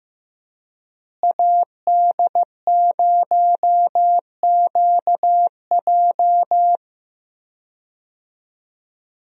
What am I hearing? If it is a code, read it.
Morse code, heard as a keyed tone, about 14 wpm, 700 Hz. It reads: AD0QJ
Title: USA Call Signs - 5 Characters - Repeated 2x - 15wpm